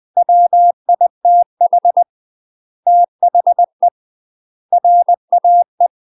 Morse code, heard as a keyed tone, about 20 wpm, 700 Hz. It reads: WITH THE RAE
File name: UK MASTER QSOs TRAINING / 20WPM / UK MASTER QSOs 0011